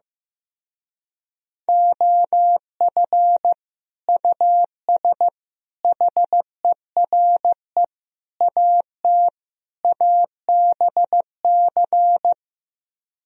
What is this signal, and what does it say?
Morse code, heard as a keyed tone, about 15 wpm, 700 Hz. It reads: OF US HERE AT ABC
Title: Speed-Racing - Sets of 5 Words 15wpm